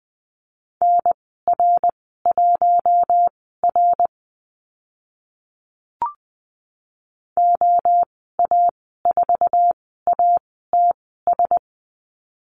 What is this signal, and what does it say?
Morse code, heard as a keyed tone, about 20 wpm, spaced as for 14 wpm, 700 Hz. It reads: NR1R OA4ATS
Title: 076 Lesson 13g - Letter L - Call Signs 20wpm